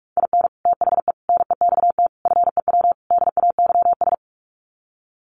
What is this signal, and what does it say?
Morse code, heard as a keyed tone, about 44 wpm, 700 Hz. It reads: IN THE NEXT FEW DAYS
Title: Sets of 5 Words Encore2 45wpm